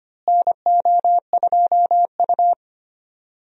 Morse code, heard as a keyed tone, about 25 wpm, 700 Hz. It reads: NO2U
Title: USA Call Signs - 4 Characters - Repeated 3x - 25wpm